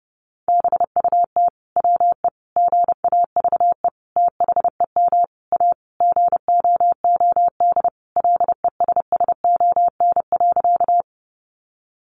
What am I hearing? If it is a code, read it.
Morse code, heard as a keyed tone, about 30 wpm, 700 Hz. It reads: BUT WE GAVE THEM A GOOD LESSON.
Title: Sentences from Top 1200 Words 30wpm